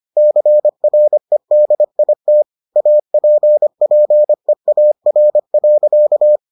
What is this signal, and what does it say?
Morse code, heard as a keyed tone, about 25 wpm, 600 Hz. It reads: CREDIT APPEAR.